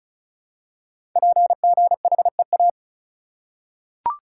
Morse code, heard as a keyed tone, about 36 wpm, 690 Hz. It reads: PGHEA
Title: Code Groups - 5 Letters - 35wpm